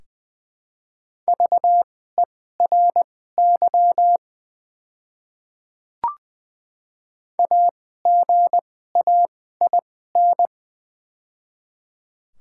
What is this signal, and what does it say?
Morse code, heard as a keyed tone, about 20 wpm, spaced as for 14 wpm, 700 Hz. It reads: VERY AGAIN
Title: 2x Character Spacing - Top 300 Words - 20wpm